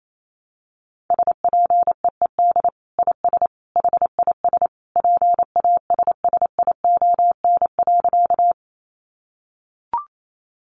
Morse code, heard as a keyed tone, about 28 wpm, 700 Hz. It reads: SPEED IS HIS PASSION.
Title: Sentences from Top 1300 Words 28wpm